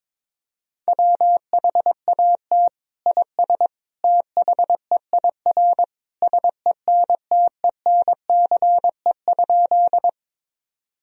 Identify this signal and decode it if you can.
Morse code, heard as a keyed tone, about 22 wpm, 700 Hz. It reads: WHAT IS THEIR SENTENCE?
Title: Sentences from Top 1000 Words - Review Part 1 - 22wpm